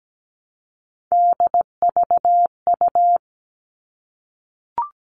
Morse code, heard as a keyed tone, about 17 wpm, 700 Hz. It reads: DVU
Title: All Cut Numbers - 3-Digit 17wpm